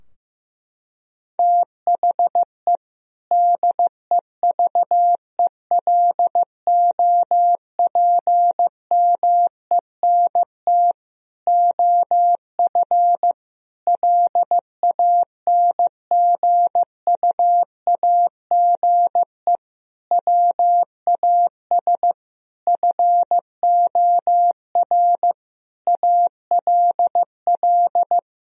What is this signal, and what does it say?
Morse code, heard as a keyed tone, about 15 wpm, 700 Hz. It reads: THE DEVELOPMENT OF LANGUAGE WAS FOR ALL
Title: Sentences from Top 1000 Words - Review Part 4 - 15wpm